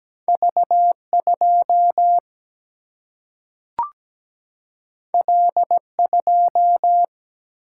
Morse code, heard as a keyed tone, about 17 wpm, 700 Hz. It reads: V 2 L 2
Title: Single-Letter Single-Number 17wpm